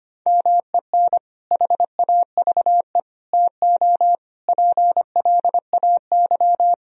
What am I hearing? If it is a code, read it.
Morse code, heard as a keyed tone, about 25 wpm, 700 Hz. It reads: MEN HAVE TO PLAY
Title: Sentences from Top 1000 Words - Review Part 3 - 25wpm